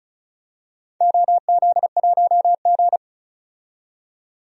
Morse code, heard as a keyed tone, about 36 wpm, 700 Hz. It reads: OZ1G
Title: DXCC-100 Call Signs - 4 Characters - 35wpm